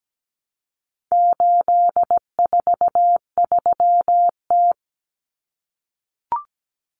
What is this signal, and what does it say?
Morse code, heard as a keyed tone, about 17 wpm, 700 Hz. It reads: 843T
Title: Common Cut Numbers - 4-Digit 17wpm